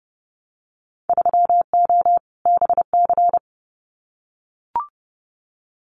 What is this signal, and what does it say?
Morse code, heard as a keyed tone, about 28 wpm, 700 Hz. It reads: 3O BC
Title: Winter Field Day Exchanges Encore 30wpm